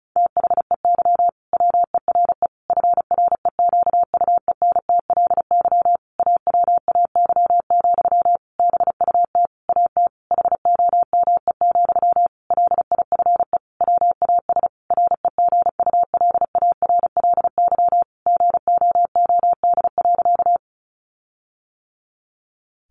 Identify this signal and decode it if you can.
Morse code, heard as a keyed tone, about 34 wpm, 700 Hz. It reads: THEY WERE FREQUENTLY AWAY, BUT AT HOME, LIFE WAS REGULARLY GOOD.
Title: Sentences from Top 1400 Words 35wpm